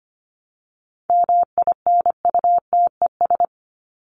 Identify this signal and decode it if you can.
Morse code, heard as a keyed tone, about 25 wpm, 700 Hz. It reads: MINUTES
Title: ES:S Suffix - Top 100 Words with Base Words 25wpm